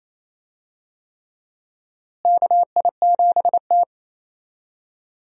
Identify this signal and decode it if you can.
Morse code, heard as a keyed tone, about 28 wpm, 700 Hz. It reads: KI7T
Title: USA Call Signs - 4 Characters - Repeated 3x - 28wpm